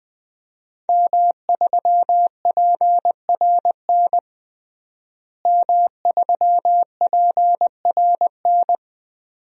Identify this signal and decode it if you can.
Morse code, heard as a keyed tone, about 20 wpm, 700 Hz. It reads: M3PRN M3PRN
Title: DXCC-100 Call Signs - 5 Characters - Repeated 2x - 20wpm